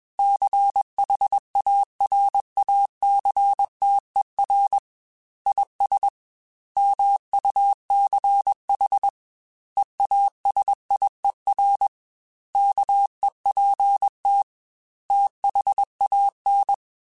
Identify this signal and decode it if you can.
Morse code, heard as a keyed tone, about 21 wpm, 800 Hz. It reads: CHARACTER IS MUCH EASIER KEPT THAN
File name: Quotes / Quotes-mp3s / Character @20